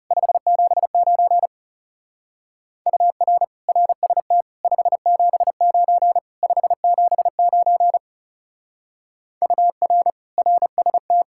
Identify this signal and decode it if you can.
Morse code, heard as a keyed tone, about 36 wpm, 700 Hz. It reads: 579 UR RST 579 579 UR RST
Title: Speed-Racing Common QSO Phrases Encore - 30wpm